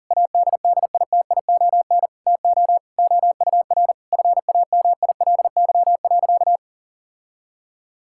Morse code, heard as a keyed tone, about 40 wpm, 700 Hz. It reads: ADDITION TO OUR FAMILY.
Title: Sentences from Top 1000 Words - Review Part 6 - 40wpm